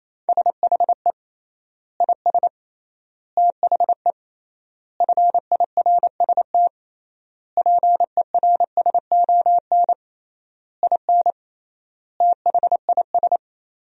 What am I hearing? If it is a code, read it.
Morse code, heard as a keyed tone, about 28 wpm, 700 Hz. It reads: SHE IS THE FIRST PERSON IN THIS
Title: 3x Word-Spacing - Sentences from Top 400 Words - 28wpm